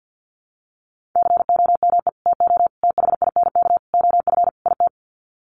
Morse code, heard as a keyed tone, about 50 wpm, 700 Hz. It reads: COME TO THINK OF IT
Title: Sets of 5 Words Encore 50wpm